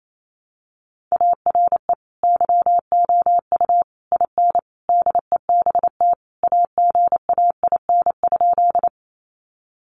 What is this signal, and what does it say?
Morse code, heard as a keyed tone, about 28 wpm, 700 Hz. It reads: ARE YOU IN DEBT AGAIN?
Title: Sentences from Top 1300 Words 28wpm